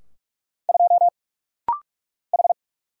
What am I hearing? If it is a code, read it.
Morse code, heard as a keyed tone, about 44 wpm, 700 Hz. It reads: 2 H
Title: Single Letter-Number Alt - Rapid-Fire 45wpm